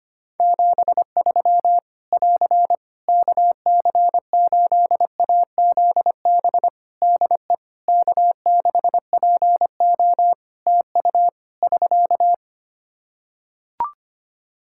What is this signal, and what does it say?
Morse code, heard as a keyed tone, about 25 wpm, 700 Hz. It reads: 73 <AR> KC8AZB DE K6PO TU <SK>
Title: Practice QSO - Exchange 4 - Ending - 25wpm